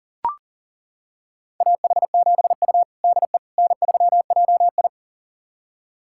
Morse code, heard as a keyed tone, about 40 wpm, 700 Hz. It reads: AH7U DE N3JI
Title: USA Callsign DE Callsign - 4 Characters - 40wpm